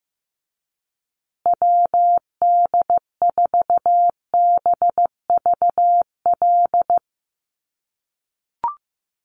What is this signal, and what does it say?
Morse code, heard as a keyed tone, about 15 wpm, 700 Hz. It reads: WD4BVL
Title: USA Call Signs Repeated 2x 15wpm